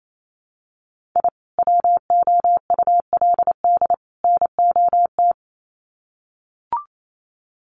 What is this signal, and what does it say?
Morse code, heard as a keyed tone, about 28 wpm, 700 Hz. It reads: I WOULD NOT
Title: Sets of 3 Words 28wpm